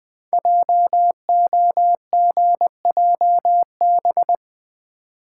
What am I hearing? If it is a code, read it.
Morse code, heard as a keyed tone, about 20 wpm, 700 Hz. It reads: JOGJB
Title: Code Groups - 5 Letters - 20wpm